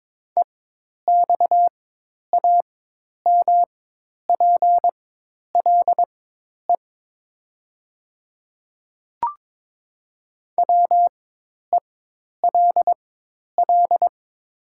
Morse code, heard as a keyed tone, about 22 wpm, spaced as for 10 wpm, 700 Hz. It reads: EXAMPLE WELL
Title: 4x Character Spacing - Top 300 Words - 22wpm